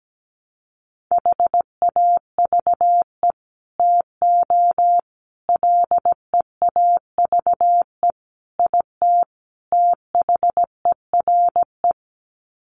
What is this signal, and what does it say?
Morse code, heard as a keyed tone, about 17 wpm, 700 Hz. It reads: HAVE TO LEAVE IT THERE
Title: Sets of 5 Words Encore2 17wpm